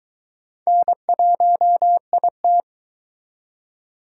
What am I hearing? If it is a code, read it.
Morse code, heard as a keyed tone, about 23 wpm, 700 Hz. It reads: N1IT